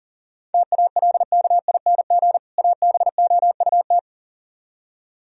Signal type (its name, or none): Morse code